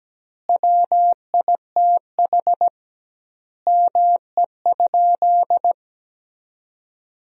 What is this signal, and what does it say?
Morse code, heard as a keyed tone, about 17 wpm, 700 Hz. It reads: WITH ME?
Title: 2x Word-Spacing - Sentences from Top 200 Words - 17wpm